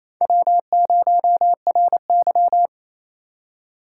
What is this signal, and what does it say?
Morse code, heard as a keyed tone, about 27 wpm, 700 Hz. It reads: W0RY
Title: USA Call Signs - 3x Character Spacing - 28wpm